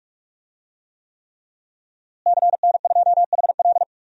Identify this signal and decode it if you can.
Morse code, heard as a keyed tone, about 43 wpm, 700 Hz. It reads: CN2HL
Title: DXCC-100 Call Signs - 5 Characters - Repeated 3x - 45wpm